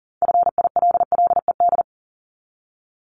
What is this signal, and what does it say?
Morse code, heard as a keyed tone, about 40 wpm, 700 Hz. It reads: FILLED
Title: ED Suffix - Top 100 Words with Base Words 40wpm